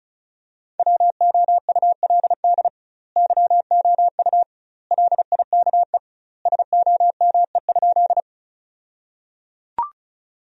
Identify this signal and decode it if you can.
Morse code, heard as a keyed tone, about 36 wpm, 700 Hz. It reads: WOULD YOU LIKE SOME?
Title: 2x Word-Spacing - Sentences from Top 100 Words - 35wpm